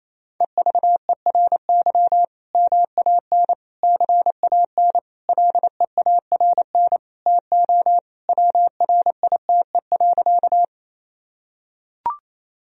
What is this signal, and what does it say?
Morse code, heard as a keyed tone, about 28 wpm, 700 Hz. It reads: EVERY MAN CAN LEARN TO WRITE.
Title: Sentences from Top 1000 Words - Review Part 1 - 28wpm